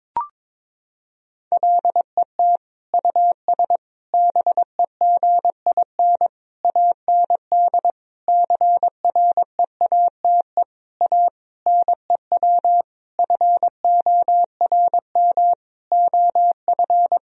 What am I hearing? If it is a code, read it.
Morse code, heard as a keyed tone, about 22 wpm, 700 Hz. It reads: LET US BEGIN AND CREATE A NEW FORM OF
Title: Sentences from Top 1000 Words - Review Part 3 - 22wpm